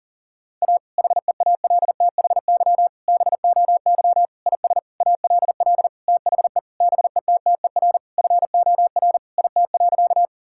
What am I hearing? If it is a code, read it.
Morse code, heard as a keyed tone, about 40 wpm, 700 Hz. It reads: A HEALTHY BOY IS ALL THE BETTER FOR IT.
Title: Sentences from Top 1000 Words - Review Part 5 - 40wpm